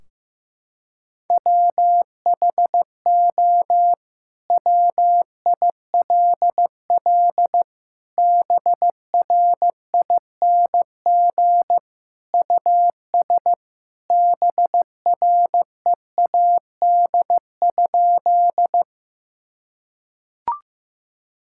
Morse code, heard as a keyed tone, about 15 wpm, 700 Hz. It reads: WHO WILL BRING US BREAD?
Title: Sentences from Top 1600 Words 15wpm